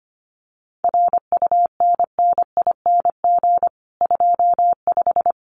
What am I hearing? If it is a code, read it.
Morse code, heard as a keyed tone, about 25 wpm, 700 Hz. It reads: RUNNING 25
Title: Common QSO Phrases 25wpm